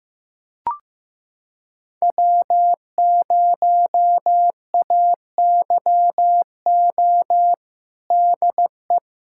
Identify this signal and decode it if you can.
Morse code, heard as a keyed tone, about 16 wpm, 700 Hz. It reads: W0AYO DE
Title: USA Callsign DE Callsign - 5 Characters - 15wpm